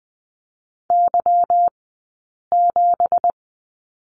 Morse code, heard as a keyed tone, about 20 wpm, spaced as for 8 wpm, 700 Hz. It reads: Y7
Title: 165 Lesson 27d - Number 7 - Two Characters 20wpm